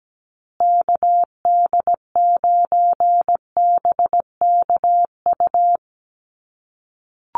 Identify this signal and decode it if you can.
Morse code, heard as a keyed tone, about 17 wpm, 700 Hz. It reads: KD9BKU
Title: USA Call Signs Repeated 2x 17wpm